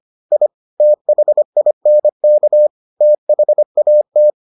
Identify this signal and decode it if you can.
Morse code, heard as a keyed tone, about 25 wpm, 600 Hz. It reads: I THINK THAT